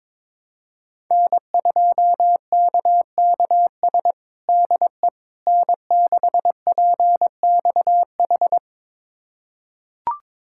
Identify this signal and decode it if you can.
Morse code, heard as a keyed tone, about 22 wpm, 700 Hz. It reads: N2KKS DE N6PXH
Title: USA Callsign DE Callsign - 5 Characters - 22wpm